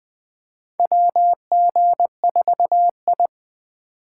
Morse code, heard as a keyed tone, about 20 wpm, 700 Hz. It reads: WG4I
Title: 2023 SOTA Chaser Call Signs - 20wpm